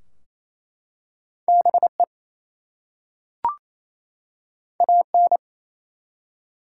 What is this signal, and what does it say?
Morse code, heard as a keyed tone, about 27 wpm, 700 Hz. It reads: BE AN
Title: Two Letter Words - 28wpm